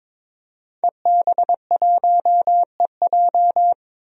Morse code, heard as a keyed tone, about 22 wpm, 700 Hz. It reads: EB1EJ
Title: DXCC-100 Call Signs - 5 Characters - 22wpm